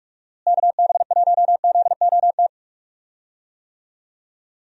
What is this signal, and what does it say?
Morse code, heard as a keyed tone, about 42 wpm, 700 Hz. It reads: KB1ZOT
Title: USA Call Signs - 6 Characters - Repeated 3x - 45wpm